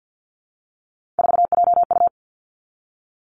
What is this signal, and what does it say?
Morse code, heard as a keyed tone, about 50 wpm, 700 Hz. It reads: 4JU
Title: Three Challenging Letter-Number Combinations 50wpm